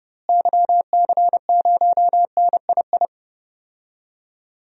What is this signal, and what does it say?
Morse code, heard as a keyed tone, about 32 wpm, 700 Hz. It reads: YC0NII
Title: DXCC-100 Call Signs - 6 Characters - Repeated 2x - 30wpm